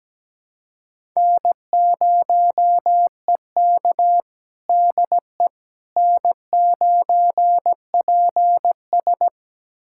Morse code, heard as a keyed tone, about 17 wpm, 700 Hz. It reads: N0EK DE N9PS
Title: USA Callsign DE Callsign - 4 Characters - 17wpm